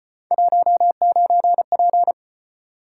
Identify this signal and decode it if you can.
Morse code, heard as a keyed tone, about 32 wpm, 700 Hz. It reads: 19P